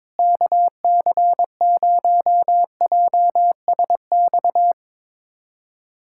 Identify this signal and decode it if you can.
Morse code, heard as a keyed tone, about 22 wpm, 700 Hz. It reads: KC0JSX
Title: USA Callsign DE Callsign 22wpm